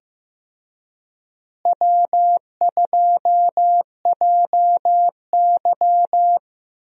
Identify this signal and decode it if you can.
Morse code, heard as a keyed tone, about 16 wpm, 700 Hz. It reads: W2JY
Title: USA Call Signs - 4 Characters - Repeated 3x - 15wpm